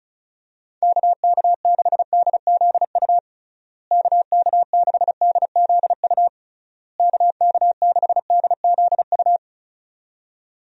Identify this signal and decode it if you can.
Morse code, heard as a keyed tone, about 34 wpm, 700 Hz. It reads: KK6DZU KK6DZU KK6DZU
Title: USA Call Signs - 6 Characters - Repeated 3x - 35wpm